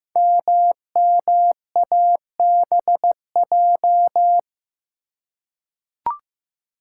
Morse code, heard as a keyed tone, about 15 wpm, 700 Hz. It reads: MMABJ
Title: Code Groups - 5 Letter:Numbers - 15wpm